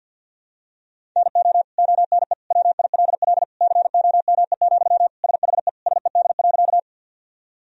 Morse code, heard as a keyed tone, about 53 wpm, 700 Hz. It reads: NO ONE WILL COME, SHE SED.
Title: Sentences from Top 100 Words and Abbreviations 50wpm